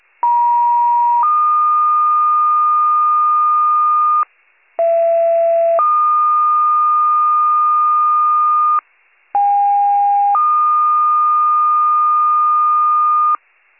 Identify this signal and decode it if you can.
Two-tone page, heard as a one-second tone, then a three-second tone, about 950 then 1220 Hz, about 670 then 1220 Hz, about 810 then 1220 Hz. It is heard from 0.2 to 13.4 s.